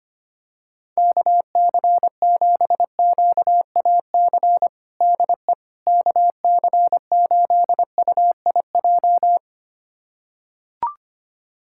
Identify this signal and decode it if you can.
Morse code, heard as a keyed tone, about 25 wpm, 700 Hz. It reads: KC7QAC DE KC8UIJ
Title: USA Callsign DE Callsign - 6 Characters - 25wpm